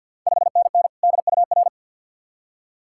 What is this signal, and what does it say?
Morse code, heard as a keyed tone, about 51 wpm, 690 Hz. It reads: 5NN DUR
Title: Ontario QSO Party Exchange 50wpm